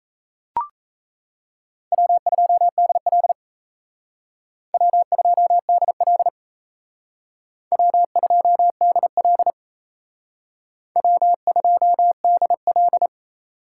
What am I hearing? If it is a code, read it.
Morse code, heard as a keyed tone, about 35 wpm, 700 Hz. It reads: W2DL W2DL W2DL W2DL
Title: Speed Racing - USA 1x2 Call Signs 28wpm